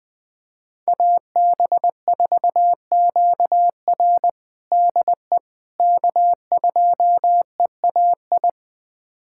Morse code, heard as a keyed tone, about 20 wpm, 700 Hz. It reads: AB4QR DE K2EAI
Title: USA Callsign DE Callsign - 5 Characters - 20wpm